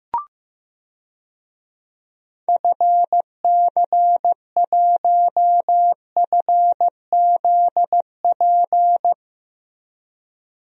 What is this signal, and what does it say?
Morse code, heard as a keyed tone, about 15 wpm, 700 Hz. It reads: FC1FZP
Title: DXCC-100 Call Signs - 6 Characters - Repeated 3x - 15wpm